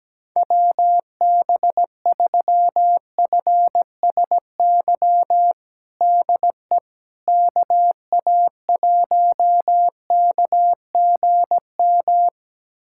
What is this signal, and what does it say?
Morse code, heard as a keyed tone, about 17 wpm, 700 Hz. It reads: WB3FSY DE KA1KGM
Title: USA Callsign DE Callsign - 6 Characters - 17wpm